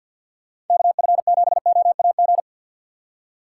Morse code, heard as a keyed tone, about 53 wpm, 700 Hz. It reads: KF7OAG
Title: USA Call Signs - 6 Characters - 8x Character Spacing - 50wpm